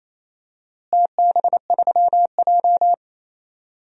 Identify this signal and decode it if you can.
Morse code, heard as a keyed tone, about 28 wpm, 700 Hz. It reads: TB3J
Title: DXCC-100 Call Signs - 4 Characters - Repeated 2x - 28wpm